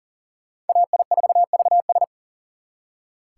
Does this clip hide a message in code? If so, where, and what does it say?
Yes: Morse code from 0.7 to 2.0 s, heard as a keyed tone, about 52 wpm, 700 Hz. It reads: AI4VS